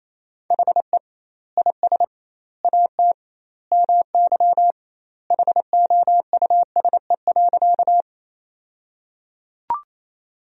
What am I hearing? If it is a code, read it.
Morse code, heard as a keyed tone, about 28 wpm, 700 Hz. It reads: HE IS AT MY HOUSE.
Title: 2x Word-Spacing - Sentences from Top 200 Words - 28wpm